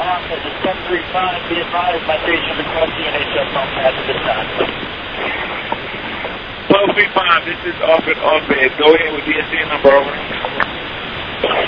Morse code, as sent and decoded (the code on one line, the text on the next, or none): none